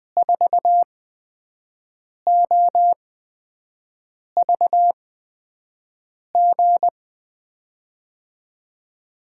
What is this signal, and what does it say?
Morse code, heard as a keyed tone, about 20 wpm, spaced as for 5 wpm, 700 Hz. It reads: 4OVG
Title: USA Call Signs - 6 Characters - 8x Character Spacing - 20wpm